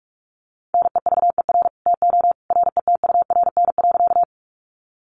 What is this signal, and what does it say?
Morse code, heard as a keyed tone, about 44 wpm, 700 Hz. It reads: NEVER TO RETURN.
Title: Sentences from Top 800 Words 45wpm